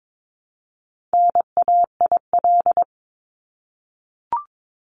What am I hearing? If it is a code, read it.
Morse code, heard as a keyed tone, about 22 wpm, 700 Hz. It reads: NAIL
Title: Top 1501-1750 Words 22wpm